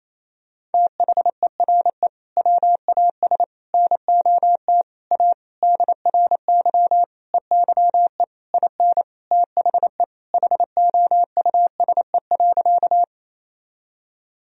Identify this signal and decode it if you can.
Morse code, heard as a keyed tone, about 27 wpm, 700 Hz. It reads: THERE WAS NOT A DRY EYE IN THE HOUSE.
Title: Sentences from Top 1000 Words - Review Part 3 - 28wpm